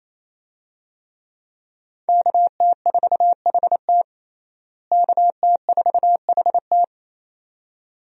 Morse code, heard as a keyed tone, about 28 wpm, 700 Hz. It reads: KT4HT KT4HT
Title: USA Call Signs - 5 Characters - Repeated 2x - 28wpm